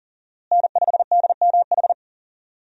Morse code, heard as a keyed tone, about 40 wpm, 700 Hz. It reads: N5DMH
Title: USA Call Signs - 5 Characters - Repeated 2x - 40wpm